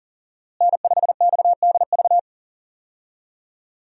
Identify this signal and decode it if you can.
Morse code, heard as a keyed tone, about 37 wpm, 700 Hz. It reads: N5XDV
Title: USA Call Signs - 5 Characters - Repeated 2x - 40wpm